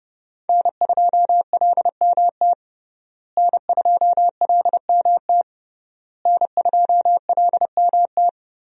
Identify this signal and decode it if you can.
Morse code, heard as a keyed tone, about 30 wpm, 700 Hz. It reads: N2LMT N2LMT N2LMT
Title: USA Call Signs - 5 Characters - Repeated 3x - 30wpm